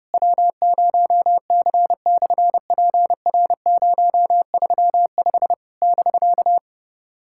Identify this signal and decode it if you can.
Morse code, heard as a keyed tone, about 30 wpm, 700 Hz. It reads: W0C/PR035 <BK>
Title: SOTA QSO Practice 30wpm